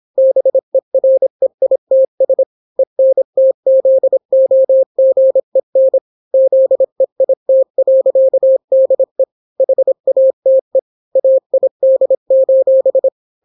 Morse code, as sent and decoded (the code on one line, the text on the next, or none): -... .-. .. - ... / . -. - --.. --- --. -. / --.. .. - .-.-.- -.. / .... .- - . / .- .. -.. ---...
BEREITS ENTZOGEN ZEIT.DE HATE AID: